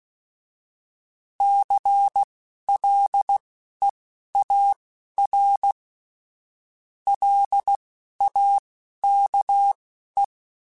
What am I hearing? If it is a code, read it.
Morse code, heard as a keyed tone, about 16 wpm, spaced as for 11 wpm, 800 Hz. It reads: CLEAR LAKE